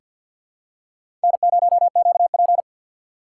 Morse code, heard as a keyed tone, about 46 wpm, 700 Hz. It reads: N0QP
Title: USA Call Signs - 4 Characters - 2x Character Spacing - 50wpm